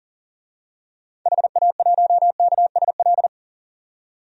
Morse code, heard as a keyed tone, about 40 wpm, 700 Hz. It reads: HA1KSL
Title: DXCC-100 Call Signs - 6 Characters - 40wpm